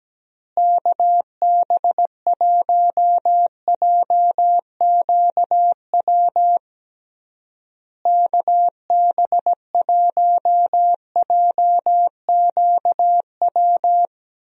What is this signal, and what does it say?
Morse code, heard as a keyed tone, about 17 wpm, 700 Hz. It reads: KB1JQW KB1JQW
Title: USA Call Signs - 6 Characters - Repeated 3x - 17wpm